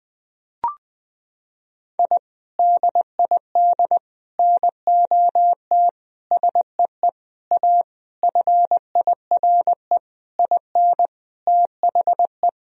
Morse code, heard as a keyed tone, about 20 wpm, 700 Hz. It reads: I DID NOT SEE A FIRE IN THE
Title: Sentences from Top 1000 Words - Review Part 2 - 20wpm